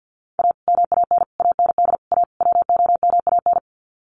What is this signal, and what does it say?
Morse code, heard as a keyed tone, about 54 wpm, 700 Hz. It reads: A MAN AND A WOMAN
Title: Sets of 5 Words Encore 50wpm